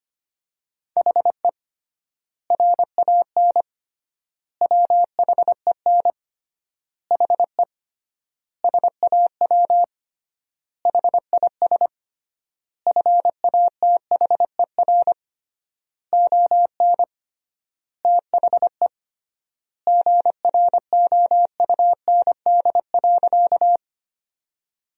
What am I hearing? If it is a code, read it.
Morse code, heard as a keyed tone, about 25 wpm, 700 Hz. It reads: HE RAN WHEN HE SAW HIS FATHER ON THE GROUND.
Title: 3x Word-Spacing - Sentences from Top 500 Words - 25wpm